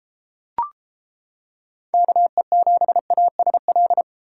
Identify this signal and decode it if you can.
Morse code, heard as a keyed tone, about 33 wpm, 700 Hz. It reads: KE7ASL